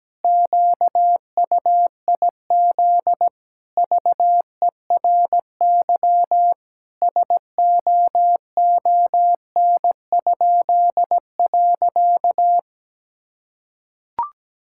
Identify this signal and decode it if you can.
Morse code, heard as a keyed tone, about 17 wpm, 700 Hz. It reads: QUIZ VERY SOON?.
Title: Pangram Challenge v2 - 17wpm